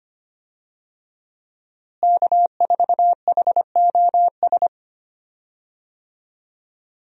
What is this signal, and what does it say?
Morse code, heard as a keyed tone, about 25 wpm, 700 Hz. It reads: K4HOS
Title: USA Call Signs - 5 Characters - Repeated 3x - 25wpm